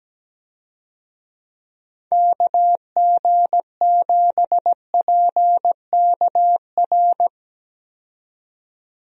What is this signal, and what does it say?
Morse code, heard as a keyed tone, about 17 wpm, 700 Hz. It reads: KG7PKR
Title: USA Call Signs - 6 Characters - 17wpm